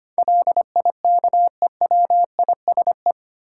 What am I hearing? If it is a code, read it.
Morse code, heard as a keyed tone, about 25 wpm, 700 Hz. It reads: LIKEWISE
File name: 8 Letter Words - First 3 Letters Given - 25wpm